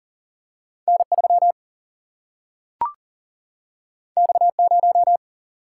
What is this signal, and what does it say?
Morse code, heard as a keyed tone, about 39 wpm, 700 Hz. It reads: N3 X0
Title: Single-Letter Single-Number 40wpm